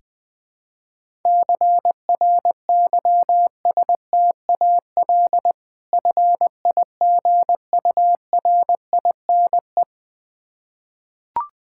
Morse code, heard as a keyed tone, about 21 wpm, 690 Hz. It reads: CRYSTAL FIGURINE
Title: Difficult Words with Hints - 20wpm